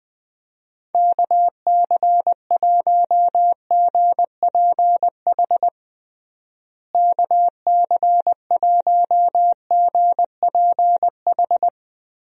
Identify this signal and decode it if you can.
Morse code, heard as a keyed tone, about 20 wpm, 700 Hz. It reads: KC1GPH KC1GPH
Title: USA Call Signs - 6 Characters - Repeated 3x - 20wpm